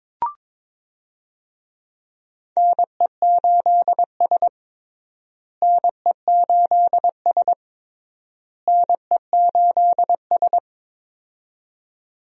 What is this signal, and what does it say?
Morse code, heard as a keyed tone, about 22 wpm, 700 Hz. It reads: NE8S NE8S NE8S
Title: USA Call Signs - 4 Characters - Repeated 3x - 22wpm